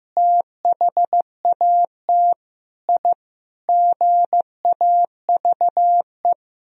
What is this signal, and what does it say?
Morse code, heard as a keyed tone, about 16 wpm, 700 Hz. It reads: THAT I GAVE
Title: Sentences from Top 1000 Words - Review Part 4 - 15wpm